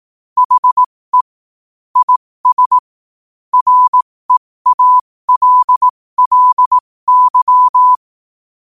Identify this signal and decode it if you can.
Morse code, heard as a keyed tone, about 18 wpm, spaced as for 15 wpm, 1000 Hz. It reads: HE IS REALLY
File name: qotdfiles / 15wpm